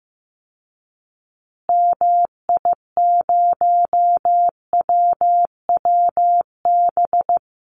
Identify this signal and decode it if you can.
Morse code, heard as a keyed tone, about 15 wpm, 700 Hz. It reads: MI0WWB
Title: Call Signs Repeated 2x - 15wpm